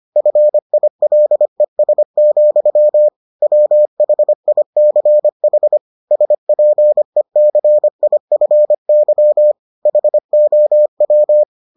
Morse code, heard as a keyed tone, about 25 wpm, 600 Hz. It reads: FILES, WHICH SPECIFY HOW